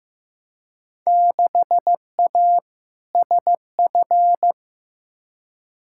Morse code, heard as a keyed tone, about 15 wpm, 700 Hz. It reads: 6A SF
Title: Field Day Exchanges - 15wpm